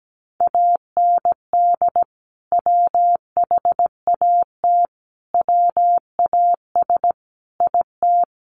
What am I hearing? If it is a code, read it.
Morse code, heard as a keyed tone, about 17 wpm, 700 Hz. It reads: AND WHAT WAS IT
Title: Sentences from Top 100 Words 17wpm